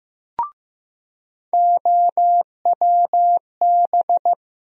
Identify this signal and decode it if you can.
Morse code, heard as a keyed tone, about 15 wpm, 700 Hz. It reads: OWB
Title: USA Call Sign Postfixes - 3 Letters 15wpm